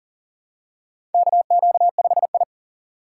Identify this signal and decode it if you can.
Morse code, heard as a keyed tone, about 41 wpm, 700 Hz. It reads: KQ5I
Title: USA Call Signs - 4 Characters - 3x Character Spacing - 40wpm